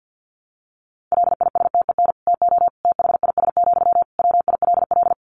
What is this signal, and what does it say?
Morse code, heard as a keyed tone, about 50 wpm, 700 Hz. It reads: LISTEN TO THIS, WILL
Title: Sentences from Top 400 Words 50wpm